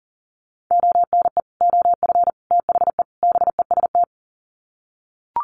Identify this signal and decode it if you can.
Morse code, heard as a keyed tone, about 37 wpm, 700 Hz. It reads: ONE OF THE BEST
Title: Sets of 4 Words 40wpm